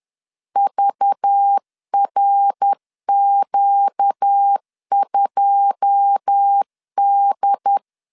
Morse code, heard as a keyed tone, about 10 wpm, 800 Hz. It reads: VRQ2D
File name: CW / DET / 1 9 1500C S B 10wpm 4439